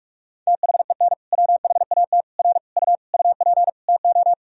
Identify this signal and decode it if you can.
Morse code, heard as a keyed tone, about 46 wpm, 700 Hz. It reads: THEN WHAT R U UP TO
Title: Sentences from Top 100 Words and Abbreviations 45wpm